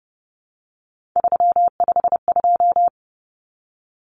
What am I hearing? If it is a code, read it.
Morse code, heard as a keyed tone, about 30 wpm, 700 Hz. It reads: 352
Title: Numbers - 3-Digit 30wpm